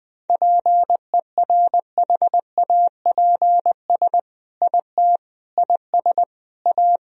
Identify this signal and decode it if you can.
Morse code, heard as a keyed tone, about 20 wpm, 700 Hz. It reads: PERHAPS IT IS A
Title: Sentences from Top 1000 Words - Review Part 3 - 20wpm